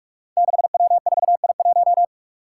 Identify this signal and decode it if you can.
Morse code, heard as a keyed tone, about 44 wpm, 700 Hz. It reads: 6W4I1
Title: Five Challenging Letter-Number Combinations 45wpm